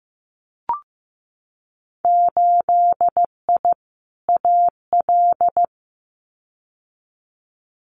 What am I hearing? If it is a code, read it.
Morse code, heard as a keyed tone, about 15 wpm, 700 Hz. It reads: 8I AL
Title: Winter Field Day Exchanges Encore 15wpm